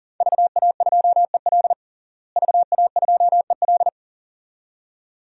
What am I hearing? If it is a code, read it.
Morse code, heard as a keyed tone, about 40 wpm, 700 Hz. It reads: VA2EL VA2EL
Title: DXCC-100 Call Signs - 5 Characters - Repeated 2x - 40wpm